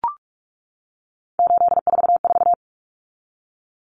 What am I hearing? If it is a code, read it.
Morse code, heard as a keyed tone, about 44 wpm, 700 Hz. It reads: 844
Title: Common Cut Numbers - 3-Digit 45wpm